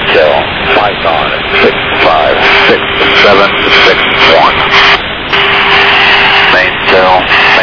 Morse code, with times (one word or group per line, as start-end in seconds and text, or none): none